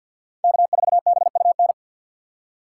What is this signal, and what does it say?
Morse code, heard as a keyed tone, about 51 wpm, 700 Hz. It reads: K4BUN
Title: USA Call Signs - 3x Character Spacing - 50wpm